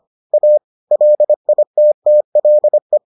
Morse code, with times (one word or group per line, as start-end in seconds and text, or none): 0.3-0.6 A
0.9-3.0 LITTLE